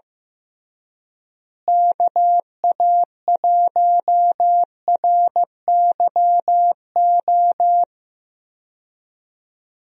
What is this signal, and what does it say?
Morse code, heard as a keyed tone, about 15 wpm, 700 Hz. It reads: KA1RYO
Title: USA Call Signs - 6 Characters - Repeated 3x - 15wpm